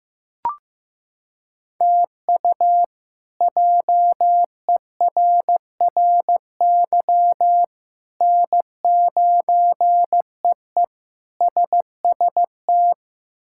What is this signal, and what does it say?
Morse code, heard as a keyed tone, about 16 wpm, 700 Hz. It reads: TU JERRY N9EE SST
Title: K1USN SST QSO Practice - 15wpm